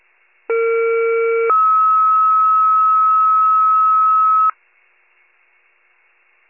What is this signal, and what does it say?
Two-tone page, heard as a one-second tone, then a three-second tone, about 460 then 1290 Hz. It is heard from 0.5 to 4.5 s.
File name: wcalert